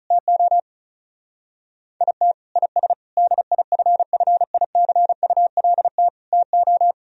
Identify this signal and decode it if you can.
Morse code, heard as a keyed tone, about 44 wpm, 700 Hz. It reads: TO IT IS DIFFICULT TO